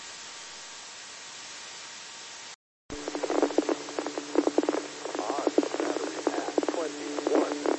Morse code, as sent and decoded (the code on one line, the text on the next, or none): none